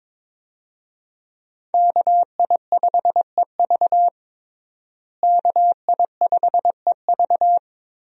Morse code, heard as a keyed tone, about 22 wpm, 700 Hz. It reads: KI5EV KI5EV
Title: USA Call Signs - 5 Characters - Repeated 3x - 22wpm